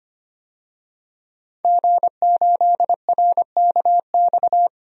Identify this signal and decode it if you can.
Morse code, heard as a keyed tone, about 25 wpm, 700 Hz. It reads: G8RKX
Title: DXCC-100 Call Signs - 5 Characters - 25wpm